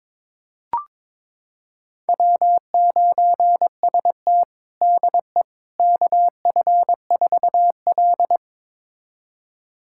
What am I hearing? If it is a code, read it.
Morse code, heard as a keyed tone, about 22 wpm, 700 Hz. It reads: W9ST DE KF4L
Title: USA Callsign DE Callsign - 4 Characters - 22wpm